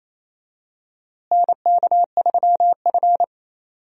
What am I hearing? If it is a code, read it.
Morse code, heard as a keyed tone, about 28 wpm, 700 Hz. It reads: NK3F
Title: USA Call Signs - 4 Characters - Repeated 3x - 28wpm